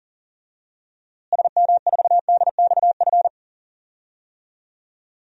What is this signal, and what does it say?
Morse code, heard as a keyed tone, about 40 wpm, 700 Hz. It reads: SM4DXF
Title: DXCC-100 Call Signs - 6 Characters - Repeated 2x - 40wpm